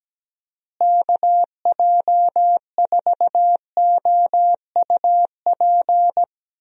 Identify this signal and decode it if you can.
Morse code, heard as a keyed tone, about 17 wpm, 700 Hz. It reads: KJ4OUP
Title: USA Call Signs - 6 Characters - Repeated 3x - 17wpm